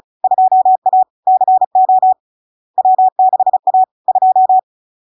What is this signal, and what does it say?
Morse code, heard as a keyed tone, about 36 wpm, 750 Hz. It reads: 2A CO W6A 2
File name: LARK-FD-2022-Log-in-CW-35wpm-750-Hz